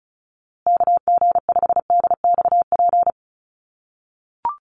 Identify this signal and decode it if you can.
Morse code, heard as a keyed tone, about 34 wpm, 700 Hz. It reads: KG5DXP
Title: USA Call Signs Repeated 3x 35wpm